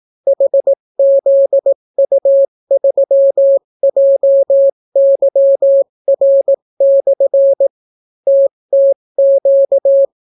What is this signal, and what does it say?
Morse code, heard as a keyed tone, about 18 wpm, 550 Hz. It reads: HZU3JYR/ TTQ